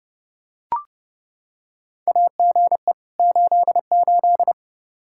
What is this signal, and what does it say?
Morse code, heard as a keyed tone, about 32 wpm, 700 Hz. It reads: AGE 88
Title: Practice QSO - Exchange 3 - More Info - 30wpm